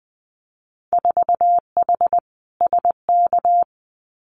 Morse code, heard as a keyed tone, about 20 wpm, 700 Hz. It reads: 4H SK
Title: Winter Field Day Exchanges Encore 20wpm